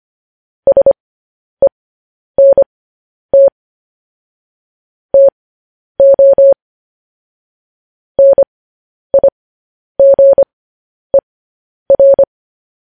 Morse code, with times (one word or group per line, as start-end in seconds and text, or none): 0.7-3.5 SENT
5.1-6.5 TO
8.2-12.2 NIGER